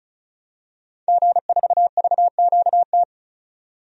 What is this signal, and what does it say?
Morse code, heard as a keyed tone, about 34 wpm, 700 Hz. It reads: G4VQT
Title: DXCC-100 Call Signs - 5 Characters - 35wpm